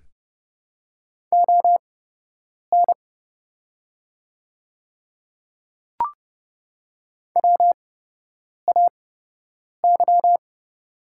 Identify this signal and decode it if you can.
Morse code, heard as a keyed tone, about 30 wpm, spaced as for 8 wpm, 700 Hz. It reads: ON WAY
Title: 8x Character Spacing - Top 100 Words - 30wpm